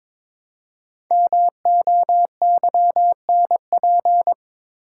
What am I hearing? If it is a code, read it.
Morse code, heard as a keyed tone, about 23 wpm, 700 Hz. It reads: MOYNP
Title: Code Groups - 5 Letter:Numbers - 22wpm